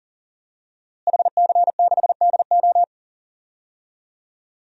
Morse code, heard as a keyed tone, about 40 wpm, 700 Hz. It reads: HC6DO
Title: DXCC-100 Call Signs - 5 Characters - Repeated 3x - 40wpm